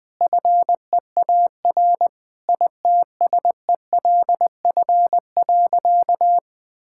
Morse code, heard as a keyed tone, about 20 wpm, 700 Hz. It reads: FEAR ITSELF.
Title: Sentences with 9-Word Phrases - 20wpm